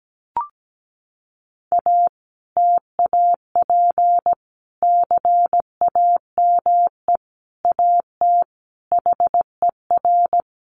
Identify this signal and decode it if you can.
Morse code, heard as a keyed tone, about 17 wpm, 700 Hz. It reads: A TAP CAME AT HER
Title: Sentences from Top 1400 Words 17wpm